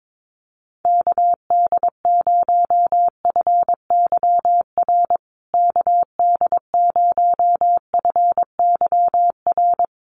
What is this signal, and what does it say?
Morse code, heard as a keyed tone, about 22 wpm, 700 Hz. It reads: KD0FYR KD0FYR
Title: USA Call Signs Repeated 2x 22wpm